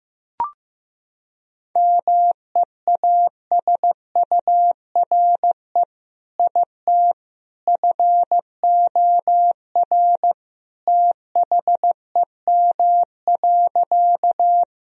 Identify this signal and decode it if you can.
Morse code, heard as a keyed tone, about 15 wpm, 700 Hz. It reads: MEASURE IT FOR THEM.
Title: Sentences from Top 1000 Words - Review Part 2 - 15wpm